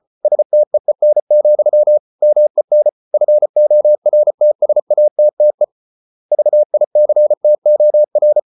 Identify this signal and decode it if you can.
Morse code, heard as a keyed tone, about 34 wpm, 600 Hz. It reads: STEEN, MEN FORTSATTE VICTOR
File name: Ferie QRQ 34wpm